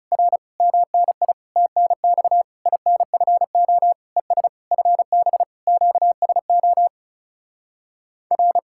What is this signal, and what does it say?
Morse code, heard as a keyed tone, about 37 wpm, 700 Hz. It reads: R MNI TNX INFO ES FB QSO R
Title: Speed-Racing Common QSO Phrases - 30wpm